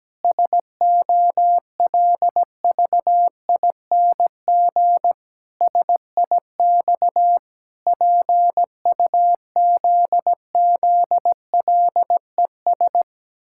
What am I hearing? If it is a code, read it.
Morse code, heard as a keyed tone, about 17 wpm, 700 Hz. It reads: SOLVING SIX PUZZLES
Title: Pangram Challenge v2 - 17wpm